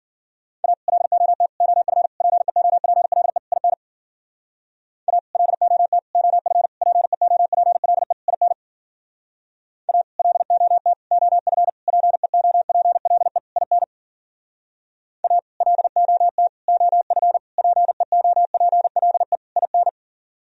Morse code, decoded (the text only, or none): A LOT OF PEOPLE IN A LOT OF PEOPLE IN A LOT OF PEOPLE IN A LOT OF PEOPLE IN